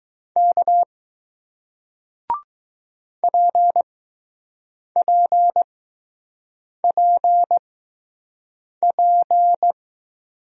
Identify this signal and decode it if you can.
Morse code, heard as a keyed tone, about 20 wpm, spaced as for 6 wpm, 700 Hz. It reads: K PPPP